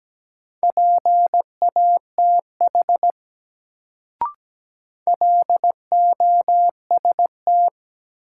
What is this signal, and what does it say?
Morse code, heard as a keyed tone, about 17 wpm, 700 Hz. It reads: PATH LOST
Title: Sentences from Top 1400 Words - New Word Drill Alt - 17wpm